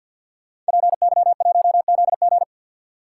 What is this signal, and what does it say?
Morse code, heard as a keyed tone, about 64 wpm, 700 Hz. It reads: PY1ZG